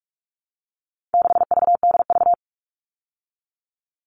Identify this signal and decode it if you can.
Morse code, heard as a keyed tone, about 44 wpm, 700 Hz. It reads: 6VDV
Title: All Cut Numbers - 4-Digit 45wpm